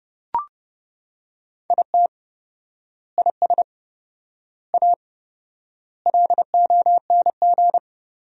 Morse code, heard as a keyed tone, about 30 wpm, 700 Hz. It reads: IT IS A LONG
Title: 4x Word-Spacing - Sentences from Top 300 Words - 30wpm